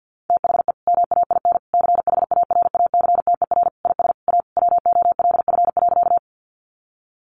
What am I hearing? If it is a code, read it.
Morse code, heard as a keyed tone, about 50 wpm, 700 Hz. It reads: THE MAIN CHARACTER IS A WOLF.
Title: Additional Easy Phrases 50wpm